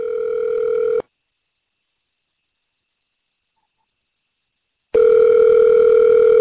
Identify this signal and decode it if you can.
Ringback tone (North America), 440 + 480 Hz.